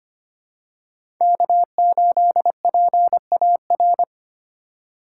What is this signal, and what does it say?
Morse code, heard as a keyed tone, about 25 wpm, 700 Hz. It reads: K8PAR
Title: USA Call Signs - 5 Characters - 2x Character Spacing - 25wpm